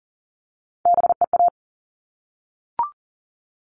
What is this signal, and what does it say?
Morse code, heard as a keyed tone, about 40 wpm, 700 Hz. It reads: BEA
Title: All Cut Numbers - 3-Digit 40wpm